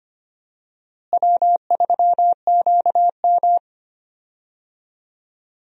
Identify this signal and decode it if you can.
Morse code, heard as a keyed tone, about 25 wpm, 700 Hz. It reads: W3QM
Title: USA Call Signs - 4 Characters - Repeated 3x - 25wpm